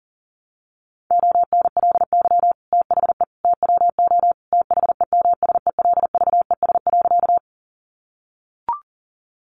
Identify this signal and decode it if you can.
Morse code, heard as a keyed tone, about 40 wpm, 700 Hz. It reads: ONLY THE TWO THEMSELVES.
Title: Sentences from Top 800 Words 40wpm